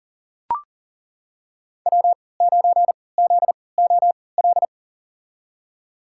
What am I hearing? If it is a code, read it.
Morse code, heard as a keyed tone, about 40 wpm, spaced as for 23 wpm, 700 Hz. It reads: W9ZOL
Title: USA Call Signs - 5 Characters - 3x Character Spacing - 40wpm